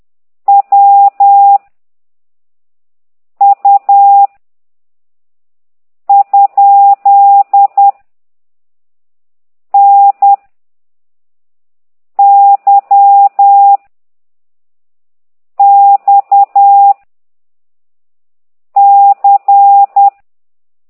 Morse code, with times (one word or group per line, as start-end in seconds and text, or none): 0.5-1.6 W
3.4-4.3 U
6.1-7.9 ?
9.7-10.3 N
12.2-13.8 Y
15.6-16.9 X
18.8-20.1 C